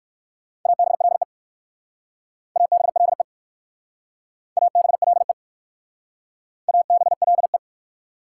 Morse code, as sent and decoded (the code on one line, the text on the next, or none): .- -... .-.. / .- -... .-.. / .- -... .-.. / .- -... .-.. .
ABLE ABLE ABLE ABLE